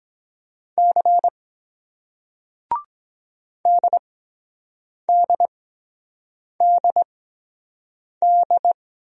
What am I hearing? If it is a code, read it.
Morse code, heard as a keyed tone, about 24 wpm, spaced as for 7 wpm, 700 Hz. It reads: C DDDD